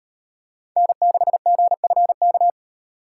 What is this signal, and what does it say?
Morse code, heard as a keyed tone, about 38 wpm, 700 Hz. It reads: N6GFK